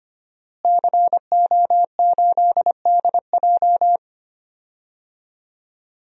Morse code, heard as a keyed tone, about 25 wpm, 700 Hz. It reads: CO8DJ
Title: DXCC-100 Call Signs - 5 Characters - Repeated 3x - 25wpm